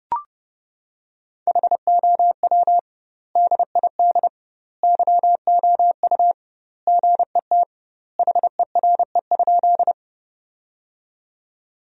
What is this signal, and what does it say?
Morse code, heard as a keyed tone, about 30 wpm, 700 Hz. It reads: HOW DID YOU GET HERE?
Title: 2x Word-Spacing - Sentences from Top 200 Words - 30wpm